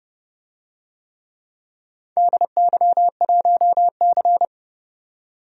Morse code, heard as a keyed tone, about 28 wpm, 700 Hz. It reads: DY1C
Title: DXCC-100 Call Signs - 4 Characters - 30wpm